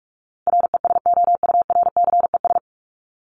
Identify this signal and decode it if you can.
Morse code, heard as a keyed tone, about 44 wpm, 700 Hz. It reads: RESOURCES
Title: ES:S Suffix - Top 100 Words with Base Words 45wpm